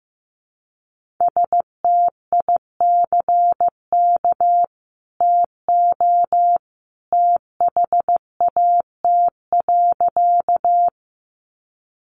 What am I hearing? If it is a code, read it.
Morse code, heard as a keyed tone, about 15 wpm, 700 Hz. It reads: STICK TO THAT.
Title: Sentences from Top 1200 Words 15wpm